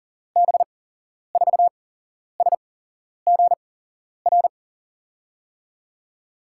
Morse code, heard as a keyed tone, about 40 wpm, spaced as for 11 wpm, 700 Hz. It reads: B4SGR
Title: USA Call Signs - 6 Characters - 8x Character Spacing - 40wpm